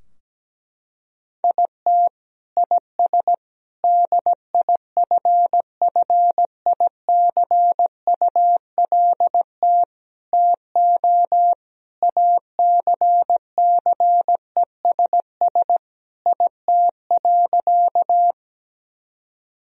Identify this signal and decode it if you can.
Morse code, heard as a keyed tone, about 17 wpm, 700 Hz. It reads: IT IS DIFFICULT TO ACCESS IT.
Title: Sentences from Top 1000 Words - Review Part 4 - 17wpm